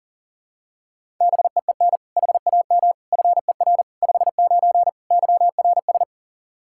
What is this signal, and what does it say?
Morse code, heard as a keyed tone, about 40 wpm, 700 Hz. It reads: BEEN HAM FER 59 YRS
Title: Practice QSO - Exchange 3 - More Info - 40wpm